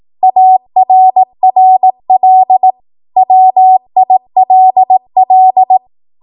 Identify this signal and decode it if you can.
Morse code, heard as a keyed tone, about 18 wpm, 750 Hz. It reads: ARRL WILL
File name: Morse / 18WPM